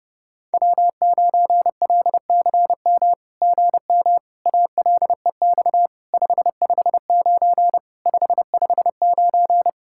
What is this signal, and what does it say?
Morse code, heard as a keyed tone, about 30 wpm, 700 Hz. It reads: W9LCM GM ALEX 559 559
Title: SOTA QSO Practice 30wpm